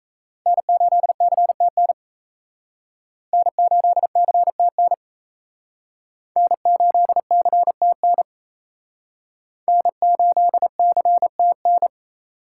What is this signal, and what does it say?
Morse code, heard as a keyed tone, about 37 wpm, 700 Hz. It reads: N8CTN N8CTN N8CTN N8CTN